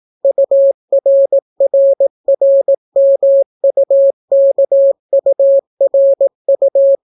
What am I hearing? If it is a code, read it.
Morse code, heard as a keyed tone, about 18 wpm, 550 Hz. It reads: URRRMUKURU